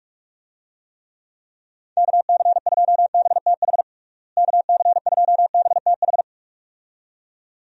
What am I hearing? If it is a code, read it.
Morse code, heard as a keyed tone, about 46 wpm, 700 Hz. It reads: KC2BTH KC2BTH
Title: USA Call Signs - 6 Characters - Repeated 2x - 45wpm